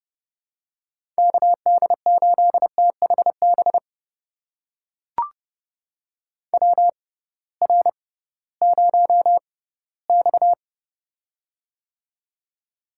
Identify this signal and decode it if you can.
Morse code, heard as a keyed tone, about 30 wpm, spaced as for 18 wpm, 700 Hz. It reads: KD8THB WR0X
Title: USA Call Signs - 6x Character Spacing - 30wpm